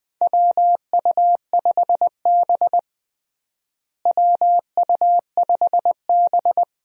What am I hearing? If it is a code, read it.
Morse code, heard as a keyed tone, about 20 wpm, 700 Hz. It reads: WU5B WU5B
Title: USA Call Signs - 4 Characters - Repeated 2x - 20wpm